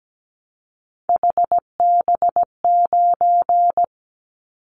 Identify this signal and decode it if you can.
Morse code, heard as a keyed tone, about 17 wpm, 700 Hz. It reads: HB9
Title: Callsign Prefixes for Top 25 DXCC - 17wpm